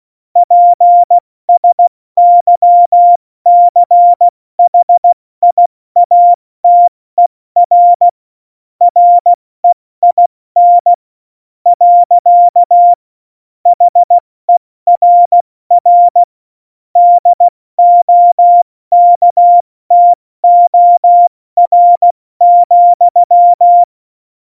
Morse code, heard as a keyed tone, about 16 wpm, spaced as for 14 wpm, 700 Hz. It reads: PSYCHIATER REIN. HERR DOKTOR,
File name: Morsekurs / Witze / Witz 06. Februar 2018, 70 F80